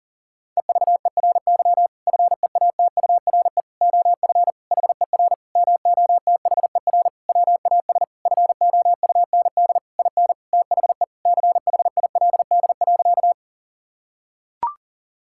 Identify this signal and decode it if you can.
Morse code, heard as a keyed tone, about 40 wpm, 700 Hz. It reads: EVERY FEATURE OF HER MOTHER WAS FOUND IN THE CHILD.
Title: Sentences from Top 1000 Words - Review Part 5 - 40wpm